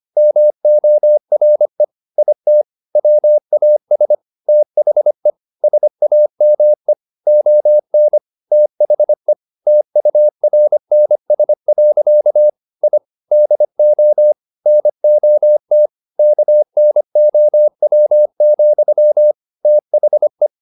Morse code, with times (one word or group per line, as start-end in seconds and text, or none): 0.2-1.8 MORE
2.2-2.6 IT
3.0-4.2 WAS
4.5-5.3 THE
5.6-6.9 SAME
7.3-8.2 ON
8.5-9.3 THE
9.7-12.5 TURNS.
12.8-13.0 I
13.3-14.3 DO
14.7-15.9 NOT
16.2-19.3 KNOW,
19.6-20.5 THE